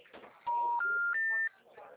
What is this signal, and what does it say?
Special information tone, with speech.